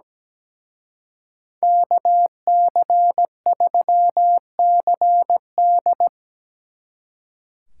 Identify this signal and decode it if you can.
Morse code, heard as a keyed tone, about 20 wpm, 700 Hz. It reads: KC3CD
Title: Speed Racing - USA Call Signs 17wpm